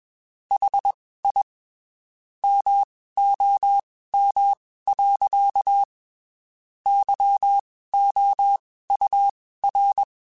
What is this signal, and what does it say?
Morse code, heard as a keyed tone, about 21 wpm, spaced as for 15 wpm, 800 Hz. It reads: HI MOM. YOUR
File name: (spaced) / QSOs Spaced-mp3s / WA7QOX 1 @20 sp